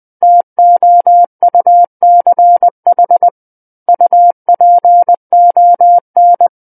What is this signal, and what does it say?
Morse code, heard as a keyed tone, about 20 wpm, 700 Hz. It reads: TOUCH UPON
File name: 20 WPM